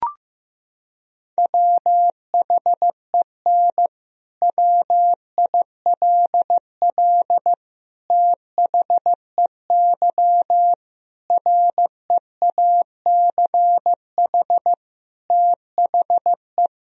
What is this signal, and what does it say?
Morse code, heard as a keyed tone, about 15 wpm, 700 Hz. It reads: WHEN WILL THEY REACH THE
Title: Sentences from Top 1000 Words - Review Part 4 - 15wpm